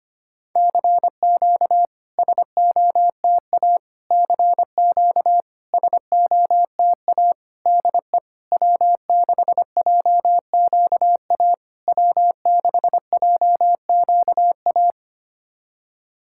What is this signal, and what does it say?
Morse code, heard as a keyed tone, about 25 wpm, 690 Hz. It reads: CQ SOTA CQ SOTA DE W6JQA W6JQA
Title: SOTA QSO Practice 25wpm